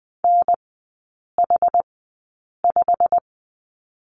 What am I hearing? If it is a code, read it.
Morse code, heard as a keyed tone, about 20 wpm, spaced as for 8 wpm, 690 Hz. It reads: NH5
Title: 087 Lesson 15e - Number 5 - Three Characters 20wpm